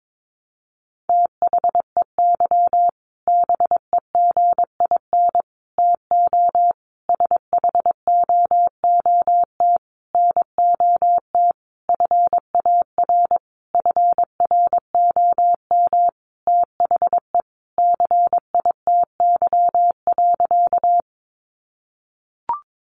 Morse code, heard as a keyed tone, about 22 wpm, 700 Hz. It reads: THEY BEGIN TO SHOOT NOT FAR FROM THE CITY.
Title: Sentences from Top 1200 Words 22wpm